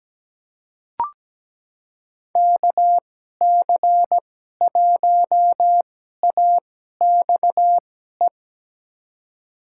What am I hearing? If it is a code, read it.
Morse code, heard as a keyed tone, about 17 wpm, spaced as for 12 wpm, 700 Hz. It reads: KC1AXE
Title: USA Call Signs - 2x Character Spacing - 17wpm